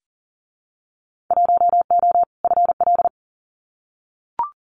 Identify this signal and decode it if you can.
Morse code, heard as a keyed tone, about 40 wpm, 700 Hz. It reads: 1O FL
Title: Winter Field Day Exchanges 40wpm